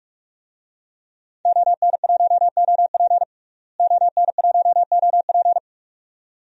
Morse code, heard as a keyed tone, about 46 wpm, 700 Hz. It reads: ON1OP ON1OP
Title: DXCC-100 Call Signs - 5 Characters - Repeated 2x - 45wpm